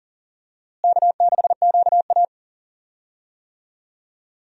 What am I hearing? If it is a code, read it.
Morse code, heard as a keyed tone, about 40 wpm, 700 Hz. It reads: K6QA
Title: USA Call Signs - 4 Characters - Repeated 2x - 40wpm